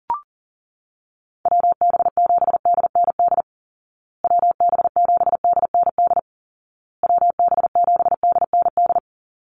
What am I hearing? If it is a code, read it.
Morse code, heard as a keyed tone, about 40 wpm, 700 Hz. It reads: WB7DND WB7DND WB7DND
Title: Call Signs Repeated 2x - 40wpm